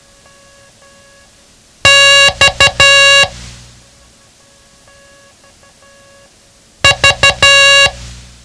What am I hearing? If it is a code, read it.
Morse code, heard as a keyed tone, about 12 wpm, 2800 Hz. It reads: X V
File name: ham radio / CodeQuick / Code Quick 07